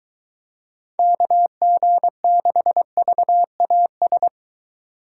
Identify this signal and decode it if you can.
Morse code, heard as a keyed tone, about 22 wpm, 700 Hz. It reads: KG6VAS